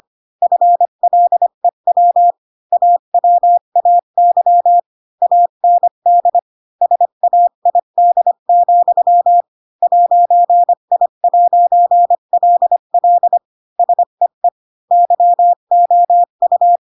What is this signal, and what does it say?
Morse code, heard as a keyed tone, about 25 wpm, 700 Hz. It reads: FLEW AWAY AND SAID, 'I'LL SEE YOU
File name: Pinocchio / Pinocchio ch7 20 25 ik1hge